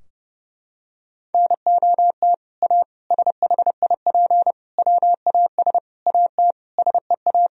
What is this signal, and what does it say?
Morse code, heard as a keyed tone, about 32 wpm, 700 Hz. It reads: NOT A SHIP WAS AT SEA
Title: Sentences from Top 1000 Words - Review Part 2 - 30wpm